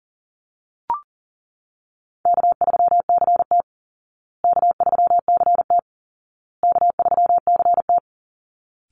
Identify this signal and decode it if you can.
Morse code, heard as a keyed tone, about 40 wpm, 700 Hz. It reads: K3CT K3CT K3CT
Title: Call Signs Repeated 2x - 40wpm